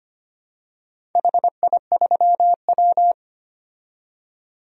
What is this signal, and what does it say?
Morse code, heard as a keyed tone, about 25 wpm, 700 Hz. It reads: HI3W
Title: DXCC-100 Call Signs - 4 Characters - 25wpm